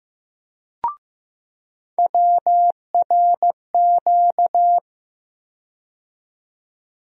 Morse code, heard as a keyed tone, about 15 wpm, 700 Hz. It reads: WRQ
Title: USA Call Sign Postfixes - 3 Letters 15wpm